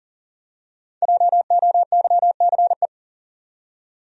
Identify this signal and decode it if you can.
Morse code, heard as a keyed tone, about 40 wpm, 700 Hz. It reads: JOYCE
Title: USA Common Names - Max 5 Letters - 40wpm